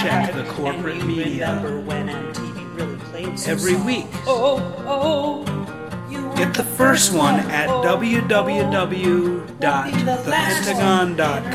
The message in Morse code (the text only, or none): none